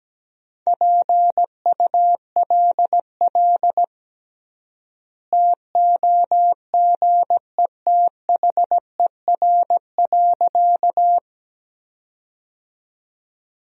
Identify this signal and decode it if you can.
Morse code, heard as a keyed tone, about 17 wpm, 700 Hz. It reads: PULL TOGETHER.
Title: 3x Word-Spacing - Sentences from Top 400 Words - 17wpm